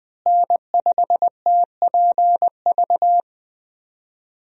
Morse code, heard as a keyed tone, about 20 wpm, 700 Hz. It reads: N5TPV
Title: USA Call Signs - 5 Characters - Repeated 2x - 20wpm